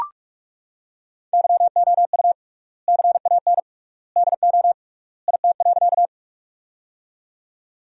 Morse code, heard as a keyed tone, about 43 wpm, 700 Hz. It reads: YOU CAN DO IT.
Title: 3x Word-Spacing - Sentences from Top 100 Words - 45wpm